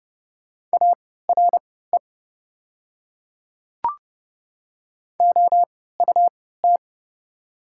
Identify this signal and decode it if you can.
Morse code, heard as a keyed tone, about 30 wpm, spaced as for 17 wpm, 700 Hz. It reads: ARE OUT
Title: 3x Character Spacing - Top 100 Words - 30wpm